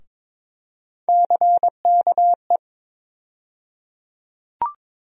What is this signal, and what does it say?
Morse code, heard as a keyed tone, about 22 wpm, 700 Hz. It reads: CKE
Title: USA Call Sign Postfixes - 3 Letters 22wpm